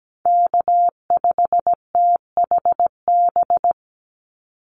Morse code, heard as a keyed tone, about 17 wpm, 700 Hz. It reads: K5THB
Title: USA Call Signs 17wpm